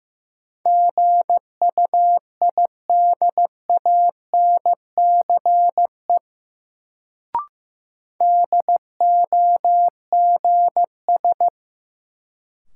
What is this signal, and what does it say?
Morse code, heard as a keyed tone, about 15 wpm, 700 Hz. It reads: GUIDANCE DOGS
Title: Sentences from Top 1600 Words - New Word Drill Alt - 15wpm